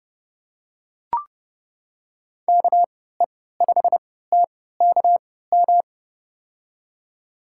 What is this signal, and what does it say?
Morse code, heard as a keyed tone, about 30 wpm, spaced as for 17 wpm, 700 Hz. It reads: KE5TKM
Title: USA Call Signs - 6 Characters - 3x Character Spacing - 30wpm